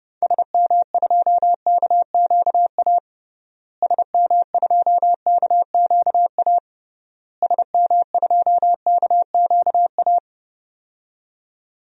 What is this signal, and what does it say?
Morse code, heard as a keyed tone, about 30 wpm, 700 Hz. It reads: SM2KQA SM2KQA SM2KQA
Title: DXCC-100 Call Signs - 6 Characters - Repeated 3x - 30wpm